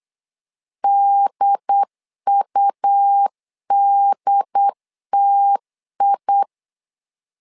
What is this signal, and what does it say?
Morse code, heard as a keyed tone, about 8 wpm, 800 Hz. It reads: DUDTI
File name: LEZIONI / 4 Q 8wpm 2732